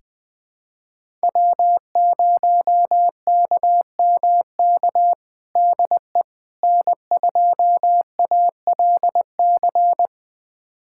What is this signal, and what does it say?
Morse code, heard as a keyed tone, about 21 wpm, 700 Hz. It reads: W0KMK DE N2ALC
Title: USA Callsign DE Callsign - 5 Characters - 20wpm